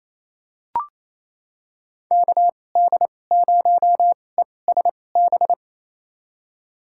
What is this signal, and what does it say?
Morse code, heard as a keyed tone, about 28 wpm, spaced as for 20 wpm, 690 Hz. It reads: KD0ESB
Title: USA Call Signs - 6 Characters - 2x Character Spacing - 28wpm